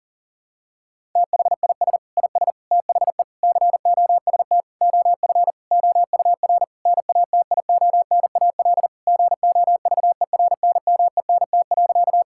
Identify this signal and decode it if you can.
Morse code, heard as a keyed tone, about 40 wpm, 700 Hz. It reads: THIS IS THE COST OF OUR NATIONAL GOVERNMENT.
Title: Sentences from Top 1000 Words - Review Part 5 - 40wpm